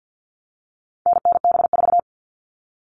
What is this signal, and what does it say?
Morse code, heard as a keyed tone, about 50 wpm, 700 Hz. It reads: NNB4
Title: All Cut Numbers - 4-Digit 50wpm